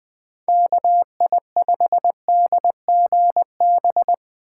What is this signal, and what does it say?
Morse code, heard as a keyed tone, about 20 wpm, 700 Hz. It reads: KI5DGB
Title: Speed Racing - USA 2x3 Call Signs 17wpm